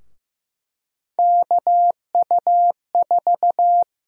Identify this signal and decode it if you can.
Morse code, heard as a keyed tone, about 15 wpm, 700 Hz. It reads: KU4
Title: USA Call Sign Prefixes - Repeated 2x - 15wpm